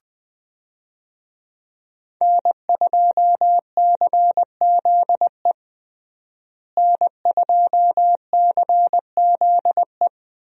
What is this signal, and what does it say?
Morse code, heard as a keyed tone, about 21 wpm, 700 Hz. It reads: N2CZE N2CZE
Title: USA Call Signs - 5 Characters - Repeated 2x - 20wpm